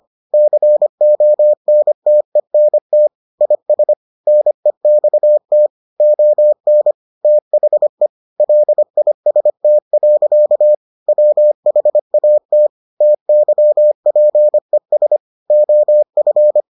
Morse code, heard as a keyed tone, about 25 wpm, 600 Hz. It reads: CONTENT IS NEXT ON THE LIST. WHAT TYPES OF